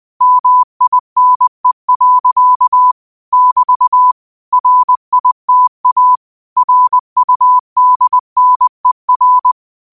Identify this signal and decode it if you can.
Morse code, heard as a keyed tone, about 20 wpm, 1000 Hz. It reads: MINE. <BT> RITA RUDNER
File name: qotdfiles / 20wpm